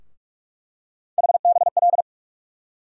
Morse code, heard as a keyed tone, about 42 wpm, 700 Hz. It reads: HBL
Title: USA Call Sign Postfixes - 3 Letters 45wpm